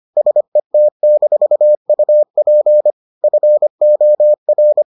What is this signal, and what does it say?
Morse code, heard as a keyed tone, about 25 wpm, 600 Hz. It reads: SET-UP FOR